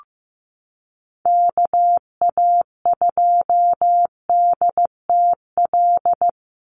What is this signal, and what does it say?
Morse code, heard as a keyed tone, about 15 wpm, 700 Hz. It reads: KA2DTL
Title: USA Call Signs 15wpm